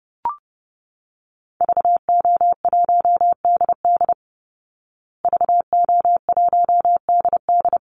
Morse code, heard as a keyed tone, about 30 wpm, 700 Hz. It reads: VO1DD VO1DD
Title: Call Signs Repeated 3x - 30wpm